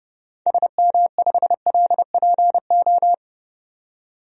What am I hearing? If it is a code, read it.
Morse code, heard as a keyed tone, about 32 wpm, 700 Hz. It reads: SM5LPO
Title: DXCC-100 Call Signs - 6 Characters - 30wpm